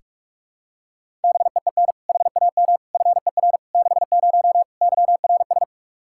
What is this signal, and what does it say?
Morse code, heard as a keyed tone, about 46 wpm, 700 Hz. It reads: BEEN HAM FER 60 YRS
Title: K1USN SST QSO Practice 45wpm